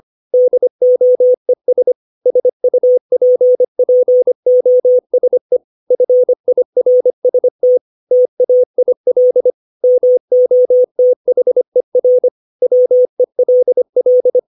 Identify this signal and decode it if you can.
Morse code, heard as a keyed tone, about 25 wpm, 500 Hz. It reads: DOES SUPPOSE FIRST TAIL MOTHER WELL
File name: 632 words / 632 Words 25WPM